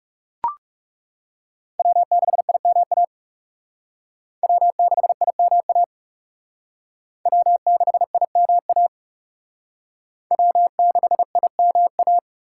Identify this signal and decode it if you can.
Morse code, heard as a keyed tone, about 37 wpm, 700 Hz. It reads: W6IMA W6IMA W6IMA W6IMA
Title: Speed Racing - USA 1x3 Call Signs 30wpm